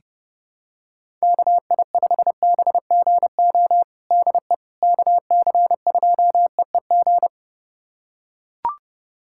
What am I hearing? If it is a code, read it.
Morse code, heard as a keyed tone, about 30 wpm, 700 Hz. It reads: KI5BGO DE KC2EEG
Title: USA Callsign DE Callsign 30wpm